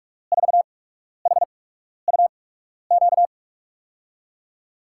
Morse code, heard as a keyed tone, about 44 wpm, spaced as for 12 wpm, 700 Hz. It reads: 4HUQ
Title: USA Call Signs - 6 Characters - 8x Character Spacing - 45wpm